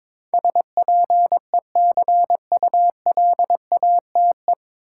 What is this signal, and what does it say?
Morse code, heard as a keyed tone, about 22 wpm, 700 Hz. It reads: SPECULATE
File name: 9 Letter Words - First 3 Letters Given - 22wpm